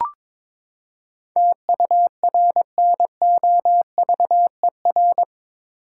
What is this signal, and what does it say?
Morse code, heard as a keyed tone, about 22 wpm, 700 Hz. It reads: TURNOVER
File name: practice / 22wpm / Eight Letter Words - 22wpm